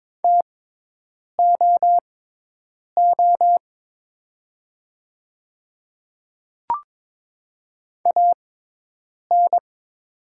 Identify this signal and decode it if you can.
Morse code, heard as a keyed tone, about 22 wpm, spaced as for 8 wpm, 700 Hz. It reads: TOO AN
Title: 6x Character Spacing - Top 300 Words - 22wpm